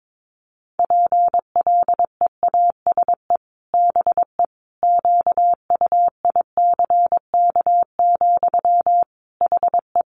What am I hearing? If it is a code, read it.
Morse code, heard as a keyed tone, about 22 wpm, 700 Hz. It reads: PLEASE BE QUICK, HE
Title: Sentences from Top 1500 Words 22wpm